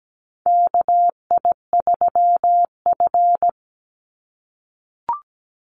Morse code, heard as a keyed tone, about 17 wpm, 700 Hz. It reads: KI3F
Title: Call Signs - 17wpm